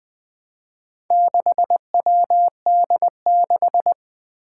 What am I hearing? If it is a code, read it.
Morse code, heard as a keyed tone, about 20 wpm, 700 Hz. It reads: 6WD6
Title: Four Challenging Letter-Number Combinations 20wpm